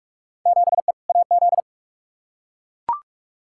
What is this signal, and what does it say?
Morse code, heard as a keyed tone, about 44 wpm, 700 Hz. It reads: 7E AZ
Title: Field Day Exchanges Encore - 45wpm